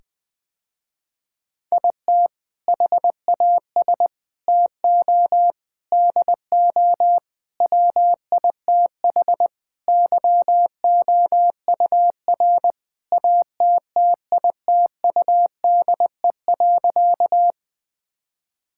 Morse code, heard as a keyed tone, about 21 wpm, 700 Hz. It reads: IT HAS TO DO WITH YOUR ATTITUDE.
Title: Sentences with 5-Word Phrases - Spoken Beginning - 20wpm